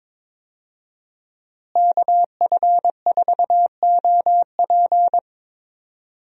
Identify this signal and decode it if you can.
Morse code, heard as a keyed tone, about 22 wpm, 700 Hz. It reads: KF4OP
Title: USA Call Signs - 5 Characters - Repeated 3x - 22wpm